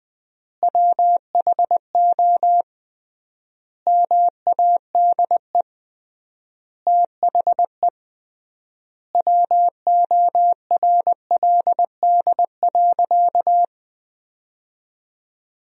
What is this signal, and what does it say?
Morse code, heard as a keyed tone, about 21 wpm, 700 Hz. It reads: WHO MADE THE WORLD.
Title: 3x Word-Spacing - Sentences from Top 200 Words - 20wpm